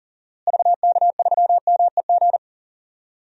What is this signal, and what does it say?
Morse code, heard as a keyed tone, about 40 wpm, 700 Hz. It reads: VK3MEG
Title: DXCC-100 Call Signs - 6 Characters - Repeated 3x - 40wpm